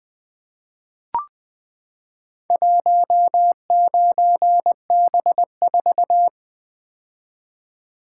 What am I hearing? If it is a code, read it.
Morse code, heard as a keyed tone, about 20 wpm, 700 Hz. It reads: 19B4
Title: Four Letter-Number Combinations - 20wpm